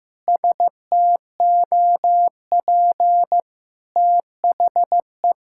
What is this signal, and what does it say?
Morse code, heard as a keyed tone, about 14 wpm, 700 Hz. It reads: STOP THE
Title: Sentences from Top 1000 Words - Review Part 6 - 15wpm